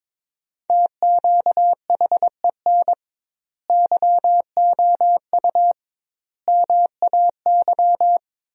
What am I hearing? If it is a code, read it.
Morse code, heard as a keyed tone, about 22 wpm, 700 Hz. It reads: TQHEN YOU MAY
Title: 2x Word-Spacing - Sentences from Top 100 Words - 22wpm